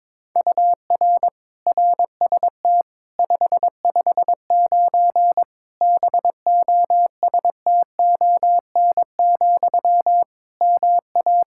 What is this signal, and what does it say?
Morse code, heard as a keyed tone, about 21 wpm, 700 Hz. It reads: UR RST 559 BOSTON, MA
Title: K1USN SST QSO Practice 22wpm